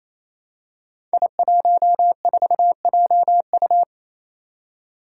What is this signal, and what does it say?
Morse code, heard as a keyed tone, about 27 wpm, 700 Hz. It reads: I14JU
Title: Five Challenging Letter-Number Combinations 28wpm